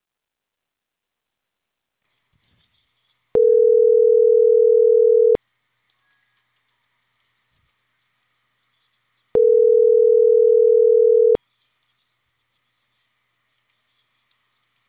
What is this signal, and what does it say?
Ringback tone (North America), 440 + 480 Hz.